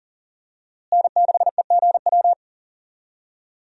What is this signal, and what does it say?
Morse code, heard as a keyed tone, about 40 wpm, 700 Hz. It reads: N6EGW
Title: USA Call Signs - 5 Characters - Repeated 3x - 40wpm